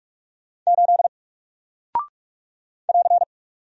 Morse code, heard as a keyed tone, about 48 wpm, 700 Hz. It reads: 8 <AR>
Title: Single Character (Letter:Number:Punctuation:Prosign) Alt 45wpm